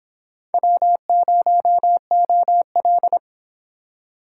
Morse code, heard as a keyed tone, about 26 wpm, 700 Hz. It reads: W0OL